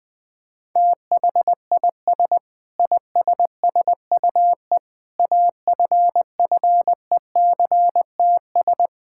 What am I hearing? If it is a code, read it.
Morse code, heard as a keyed tone, about 20 wpm, 700 Hz. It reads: THIS ISSUE AFFECTS
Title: Sentences with 8-Word Phrases - 20wpm